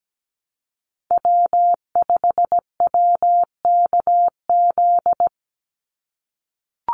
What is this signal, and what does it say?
Morse code, heard as a keyed tone, about 17 wpm, 700 Hz. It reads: W5WKZ
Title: USA Call Signs 17wpm